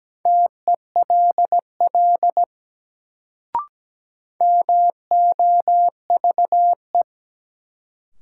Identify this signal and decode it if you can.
Morse code, heard as a keyed tone, about 17 wpm, 700 Hz. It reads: TELL MOVE
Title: Sentences from Top 200 Words - New Word Drill - 17wpm